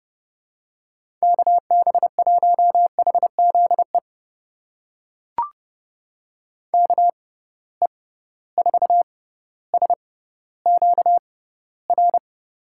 Morse code, heard as a keyed tone, about 30 wpm, spaced as for 15 wpm, 700 Hz. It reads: KB1HZE KE4SQR
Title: USA Call Signs - 6 Characters - 6x Character Spacing - 30wpm